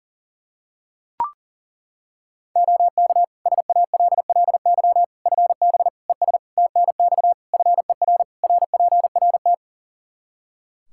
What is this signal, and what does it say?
Morse code, heard as a keyed tone, about 40 wpm, 700 Hz. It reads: OK SALLY FB ES TNX FER RPRT
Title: Practice QSO - Exchange 2 - Info - 40wpm